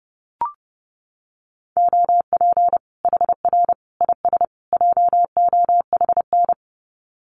Morse code, heard as a keyed tone, about 30 wpm, 700 Hz. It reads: OP HR IS JOHN
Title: Common QSO Phrases 30wpm